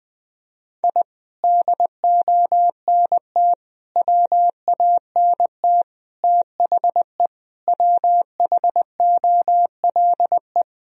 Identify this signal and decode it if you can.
Morse code, heard as a keyed tone, about 20 wpm, 700 Hz. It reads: I DONT WANT THE WHOLE
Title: Sentences from Top 1000 Words - Review Part 2 - 20wpm